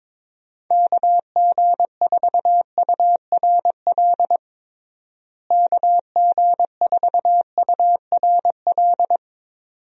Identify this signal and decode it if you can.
Morse code, heard as a keyed tone, about 22 wpm, 700 Hz. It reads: KG4URL KG4URL
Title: USA Call Signs - 6 Characters - Repeated 2x - 22wpm